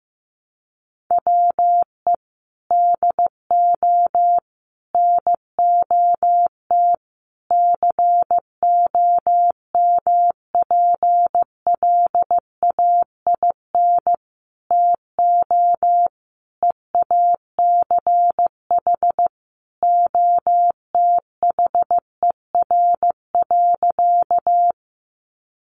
Morse code, heard as a keyed tone, about 15 wpm, 700 Hz. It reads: WE DO NOT COMPLAIN TO EACH OTHER.
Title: Sentences from Top 1300 Words 15wpm